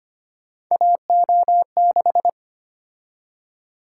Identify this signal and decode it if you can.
Morse code, heard as a keyed tone, about 25 wpm, 700 Hz. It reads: AO6
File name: Three Letter-Number Combinations - 25wpm